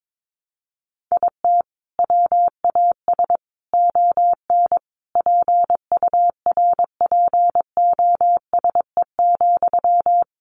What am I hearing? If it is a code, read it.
Morse code, heard as a keyed tone, about 22 wpm, 700 Hz. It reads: IT WAS ON PURPOSE,
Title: Sentences from Top 800 Words 22wpm